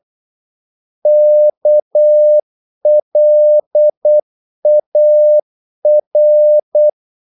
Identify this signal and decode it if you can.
Morse code, heard as a keyed tone, about 8 wpm, 600 Hz. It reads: KLAR
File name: Korte ord DK 8wpm